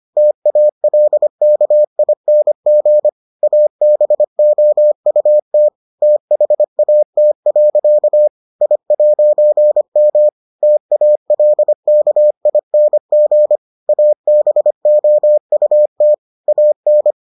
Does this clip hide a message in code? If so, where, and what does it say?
Yes: Morse code from 0.2 to 17.1 s, heard as a keyed tone, about 25 wpm, 600 Hz. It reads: TALKING ABOUT THAT. I'M TALKING ABOUT AN